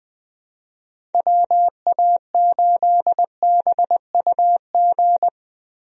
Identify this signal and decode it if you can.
Morse code, heard as a keyed tone, about 20 wpm, 700 Hz. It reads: WA8BUG
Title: USA Call Signs - 6 Characters - Repeated 2x - 20wpm